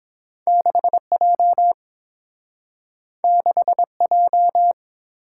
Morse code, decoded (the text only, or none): 6J 6J